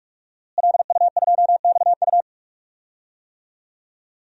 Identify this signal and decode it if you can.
Morse code, heard as a keyed tone, about 46 wpm, 700 Hz. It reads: LU2XU
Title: DXCC-100 Call Signs - 5 Characters - Repeated 2x - 45wpm